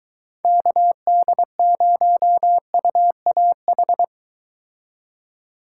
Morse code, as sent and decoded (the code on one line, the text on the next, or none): -.- -.. ----- ..- .- ....
KD0UAH